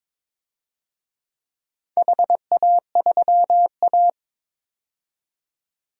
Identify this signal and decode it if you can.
Morse code, heard as a keyed tone, about 22 wpm, 700 Hz. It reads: HA3A
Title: DXCC-100 Call Signs - 4 Characters - 22wpm